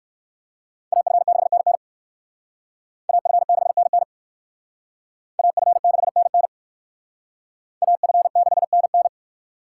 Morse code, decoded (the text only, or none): AF6NN AF6NN AF6NN AF6NN